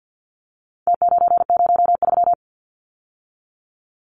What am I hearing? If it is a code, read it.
Morse code, heard as a keyed tone, about 48 wpm, 700 Hz. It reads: T903
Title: Numbers - Contest Style - 4-Digit 50wpm